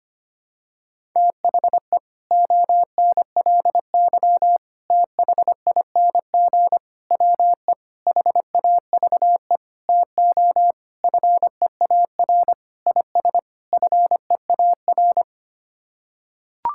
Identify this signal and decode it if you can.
Morse code, heard as a keyed tone, about 25 wpm, 700 Hz. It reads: THE ONLY THING WE HAVE TO FEAR IS FEAR
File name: Sets of 9 Words - 25wpm